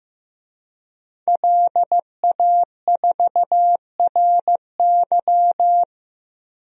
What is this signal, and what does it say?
Morse code, heard as a keyed tone, about 15 wpm, 700 Hz. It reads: LA4RY
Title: DXCC-100 Call Signs - 5 Characters - Repeated 3x - 15wpm